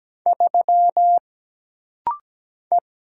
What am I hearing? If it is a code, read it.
Morse code, heard as a keyed tone, about 17 wpm, 700 Hz. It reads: 3 E
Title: Single Letter-Number Alt - Rapid-Fire 17wpm